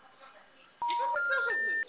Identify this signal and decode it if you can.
Special information tone, with speech.